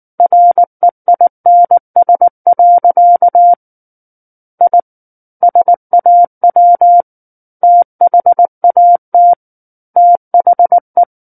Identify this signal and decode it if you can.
Morse code, heard as a keyed tone, about 19 wpm, 700 Hz. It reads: REINS. I SAW THAT THE
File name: Book One / 10-Chapter 10 - 19 WPM